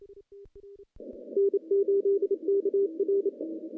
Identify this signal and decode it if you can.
Morse code, heard as a keyed tone, about 28 wpm, 410 Hz. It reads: N8KR
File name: contests / CW-SS-k5tr- -0100